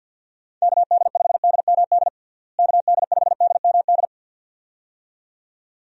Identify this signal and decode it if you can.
Morse code, heard as a keyed tone, about 46 wpm, 700 Hz. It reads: KD5DMD KD5DMD
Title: USA Call Signs - 6 Characters - Repeated 2x - 50wpm